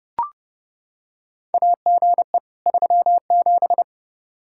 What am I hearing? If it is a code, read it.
Morse code, heard as a keyed tone, about 30 wpm, 700 Hz. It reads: AGE 37
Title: Practice QSO - Exchange 3 - More Info - 30wpm